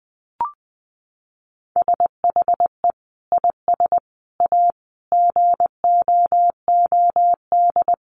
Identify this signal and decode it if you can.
Morse code, heard as a keyed tone, about 20 wpm, 700 Hz. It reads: SHE IS A GOOD
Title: Sentences from Top 1100 Words 20wpm